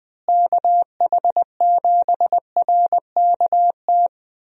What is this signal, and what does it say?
Morse code, heard as a keyed tone, about 20 wpm, 700 Hz. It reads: KH7RKT
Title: USA Call Signs - 6 Characters - Repeated 3x - 20wpm